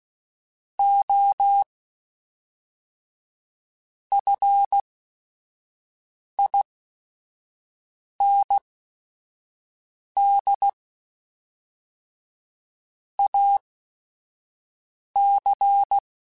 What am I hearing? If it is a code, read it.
Morse code, heard as a keyed tone, about 16 wpm, spaced as for 5 wpm, 800 Hz. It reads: O FIND AC
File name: Technical Subjects / Technical Subjects-mp3s / Sw Pwr Sup @5